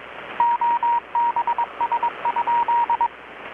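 Morse code, heard as a keyed tone, about 22 wpm, 940 Hz, with radio noise. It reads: OBS?
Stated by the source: Morse from KPH's Night of Nights XX maritime radiotelegraphy event, received in Chiapas, Mexico about 3,800 km away on a Kenwood R-600 with a grounded 10 m long-wire antenna.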